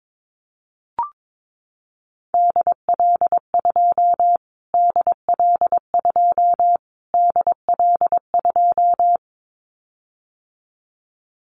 Morse code, heard as a keyed tone, about 22 wpm, 700 Hz. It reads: DL2 DL2 DL2
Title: Callsign Prefixes for Top 25 DXCC - Repeated 3x - 22wpm